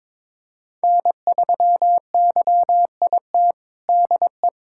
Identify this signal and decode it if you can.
Morse code, heard as a keyed tone, about 23 wpm, 700 Hz. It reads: N3YIT DE
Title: USA Callsign DE Callsign - 5 Characters - 22wpm